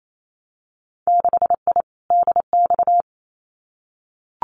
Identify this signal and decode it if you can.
Morse code, heard as a keyed tone, about 28 wpm, 700 Hz. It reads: 6I DX
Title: Winter Field Day Exchanges Encore 28wpm